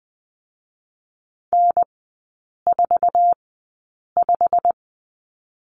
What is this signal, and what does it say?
Morse code, heard as a keyed tone, about 19 wpm, spaced as for 8 wpm, 690 Hz. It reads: N45
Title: 087 Lesson 15e - Number 5 - Three Characters 20wpm